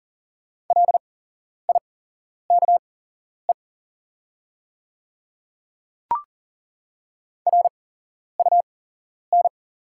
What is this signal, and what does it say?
Morse code, heard as a keyed tone, about 40 wpm, spaced as for 11 wpm, 690 Hz. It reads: LIKE RUN
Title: 8x Character Spacing - Top 300 Words - 40wpm